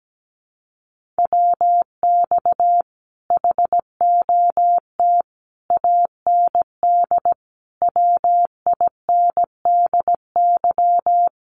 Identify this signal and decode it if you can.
Morse code, heard as a keyed tone, about 17 wpm, 700 Hz. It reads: WX HOT AND WINDY
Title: Common QSO Phrases Encore 17wpm